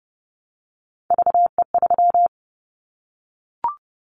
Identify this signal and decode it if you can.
Morse code, heard as a keyed tone, about 30 wpm, 690 Hz. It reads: VE3
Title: Callsign Prefixes for Top 25 DXCC - Repeated 3x - 30wpm